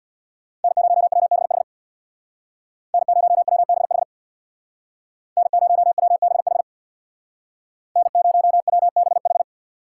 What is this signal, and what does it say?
Morse code, heard as a keyed tone, about 61 wpm, 700 Hz. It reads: N0WBH N0WBH N0WBH N0WBH
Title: Speed Racing - USA 1x3 Call Signs 50wpm